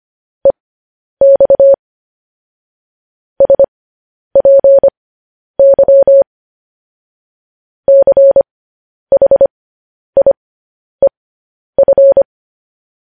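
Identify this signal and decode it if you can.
Morse code, heard as a keyed tone, about 25 wpm, spaced as for 10 wpm, 550 Hz. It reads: EX SPY CHIEF
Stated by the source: Morse code news headlines sent at 10 wpm for practice.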